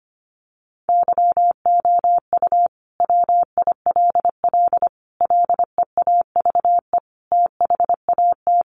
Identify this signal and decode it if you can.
Morse code, heard as a keyed tone, about 25 wpm, 700 Hz. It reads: YOU WILL LEAVE THAT
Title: Sentences from Top 900 Words 25wpm